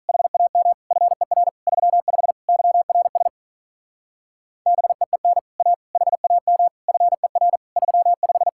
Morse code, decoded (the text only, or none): HAM FER 35 YRS BEEN A HAM FER 35